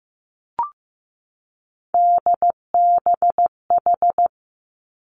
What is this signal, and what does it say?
Morse code, heard as a keyed tone, about 15 wpm, 700 Hz. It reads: DBH
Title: Three Challenging Letter-Number Combinations 15wpm